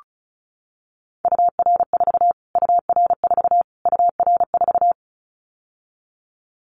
Morse code, heard as a keyed tone, about 34 wpm, 700 Hz. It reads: UR4 UR4 UR4
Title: Callsign Prefixes for Top 25 DXCC - Repeated 3x - 35wpm